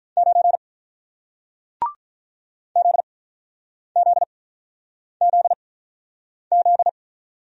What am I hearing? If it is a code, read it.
Morse code, heard as a keyed tone, about 48 wpm, spaced as for 9 wpm, 700 Hz. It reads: 9 ZZZZ